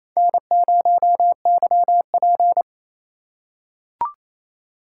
Morse code, heard as a keyed tone, about 28 wpm, 700 Hz. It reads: N0YP
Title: USA Call Signs - 4 Characters - 3x Character Spacing - 28wpm